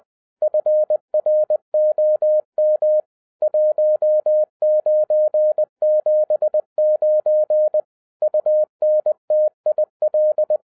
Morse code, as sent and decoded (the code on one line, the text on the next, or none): ..-. .-. --- -- / .---- ----. --... ----. / ..- -. - .. .-..
FROM 1979 UNTIL